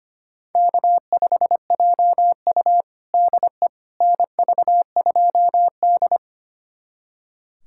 Morse code, heard as a keyed tone, about 25 wpm, 700 Hz. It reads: K5JU DE NV2D
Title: USA Callsign DE Callsign - 4 Characters - 25wpm